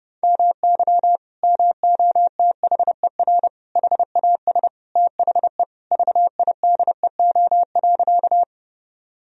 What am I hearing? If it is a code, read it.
Morse code, heard as a keyed tone, about 30 wpm, 700 Hz. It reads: MY MOTHER HAS THE VIDEO.
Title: Sentences from Top 1000 Words - Review Part 4 - 30wpm